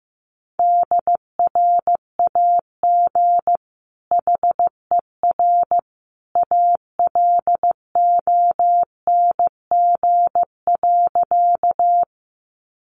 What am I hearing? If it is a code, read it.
Morse code, heard as a keyed tone, about 15 wpm, 700 Hz. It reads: DRAG HER ALONG.
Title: Sentences from Top 1500 Words 15wpm